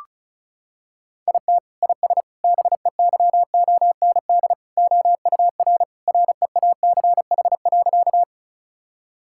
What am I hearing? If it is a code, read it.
Morse code, heard as a keyed tone, about 36 wpm, 700 Hz. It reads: IT IS BEYOND OUR REACH.
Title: Sentences from Top 1000 Words - Review Part 6 - 35wpm